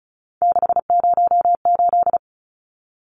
Morse code, heard as a keyed tone, about 34 wpm, 700 Hz. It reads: 608
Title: Numbers - 3-Digit 35wpm